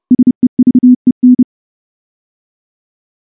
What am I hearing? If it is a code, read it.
Morse code, heard as a keyed tone, about 30 wpm, 250 Hz. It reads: SEVEN
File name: morse-training-30WPM